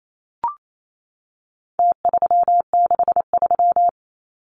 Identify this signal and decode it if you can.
Morse code, heard as a keyed tone, about 28 wpm, 700 Hz. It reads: T363
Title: Numbers - Contest Style - 4-Digit 28wpm